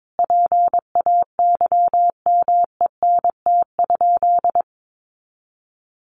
Morse code, heard as a keyed tone, about 22 wpm, 700 Hz. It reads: PAYMENT?
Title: Sentences from Top 1200 Words 22wpm